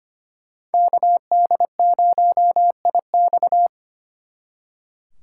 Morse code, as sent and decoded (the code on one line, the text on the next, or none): -.- -.. ----- .. -..-
KD0IX